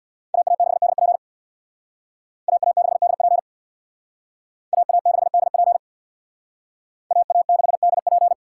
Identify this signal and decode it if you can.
Morse code, heard as a keyed tone, about 60 wpm, 700 Hz. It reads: AA6DP AA6DP AA6DP AA6DP
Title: Speed Racing - USA 2x2 Call Signs 50wpm